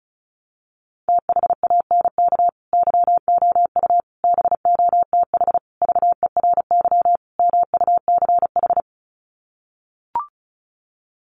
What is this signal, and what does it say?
Morse code, heard as a keyed tone, about 34 wpm, 700 Hz. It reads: THANK YOU BOTH VERY MUCH
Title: Sets of 5 Words 35wpm